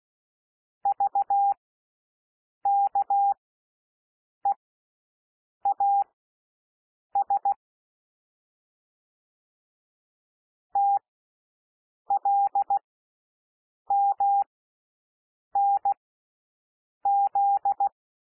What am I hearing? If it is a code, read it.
Morse code, heard as a keyed tone, about 16 wpm, spaced as for 6 wpm, 800 Hz. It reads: VKEAS TLMNZ
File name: cwmp3 / L1 SOUND LETTERS 16 wpm